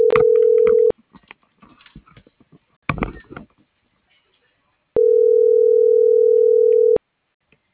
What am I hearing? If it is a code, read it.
Ringback tone (North America), 440 + 480 Hz.
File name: CENTRAL MAERSK / out sandovalmsk - .70603